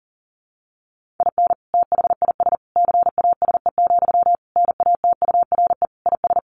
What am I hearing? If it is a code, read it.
Morse code, heard as a keyed tone, about 40 wpm, 700 Hz. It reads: IN THIS CASE, NATURE IS
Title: Sentences from Top 1400 Words 40wpm